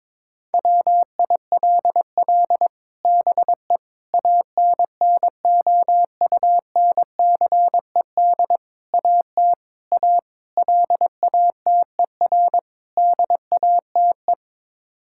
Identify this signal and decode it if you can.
Morse code, heard as a keyed tone, about 22 wpm, 700 Hz. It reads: WILL BE ANNOUNCED AT A LATER DATE
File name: practice / 22wpm / Sets of 7 Words - 22wpm